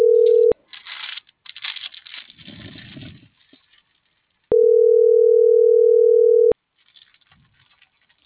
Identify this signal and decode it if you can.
Ringback tone (North America), 440 + 480 Hz.